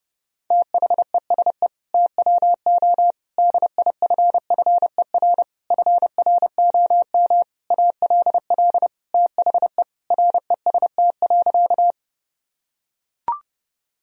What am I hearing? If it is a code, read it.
Morse code, heard as a keyed tone, about 32 wpm, 700 Hz. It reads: THESE TWO DIFFER FROM ALL THE REST.
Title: Sentences from Top 1000 Words - Review Part 1 - 30wpm